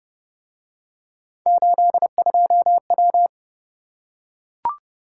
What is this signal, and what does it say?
Morse code, heard as a keyed tone, about 29 wpm, 700 Hz. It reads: 82W